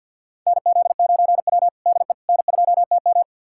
Morse code, heard as a keyed tone, about 51 wpm, 700 Hz. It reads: NZ9W DE N2TM
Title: USA Callsign DE Callsign - 4 Characters - 50wpm